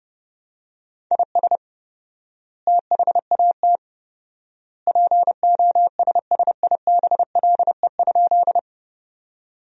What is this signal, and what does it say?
Morse code, heard as a keyed tone, about 32 wpm, 700 Hz. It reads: IS THAT POSSIBLE?
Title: 4x Word-Spacing - Sentences from Top 500 Words - 30wpm